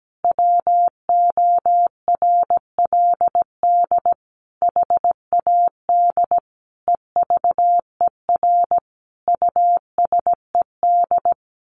Morse code, heard as a keyed tone, about 17 wpm, 700 Hz. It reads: WORLD HAD EVER USED
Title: Sentences from Top 1100 Words 17wpm